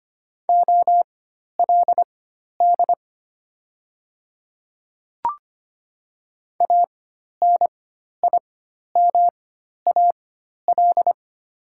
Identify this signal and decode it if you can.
Morse code, heard as a keyed tone, about 25 wpm, spaced as for 12 wpm, 700 Hz. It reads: OLD ANIMAL
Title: 4x Character Spacing - Top 200 Words - 25wpm